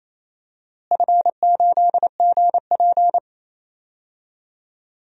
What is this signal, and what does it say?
Morse code, heard as a keyed tone, about 28 wpm, 700 Hz. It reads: F8GP
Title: DXCC-100 Call Signs - 4 Characters - Repeated 2x - 28wpm